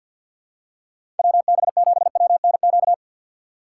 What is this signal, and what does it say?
Morse code, heard as a keyed tone, about 49 wpm, 700 Hz. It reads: WB7WNQ
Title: USA Call Signs - 6 Characters - 6x Character Spacing - 50wpm